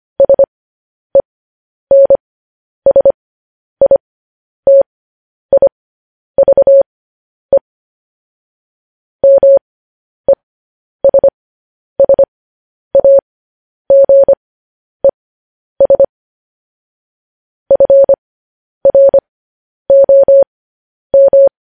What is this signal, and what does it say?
Morse code, heard as a keyed tone, about 25 wpm, spaced as for 10 wpm, 550 Hz. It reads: SENSITIVE MESSAGES FROM
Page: news-headlines-in-morse-code-at-10-wpm